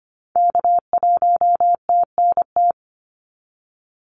Morse code, heard as a keyed tone, about 25 wpm, 700 Hz. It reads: K1TNT
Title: NA SOTA Activator Callsigns 25wpm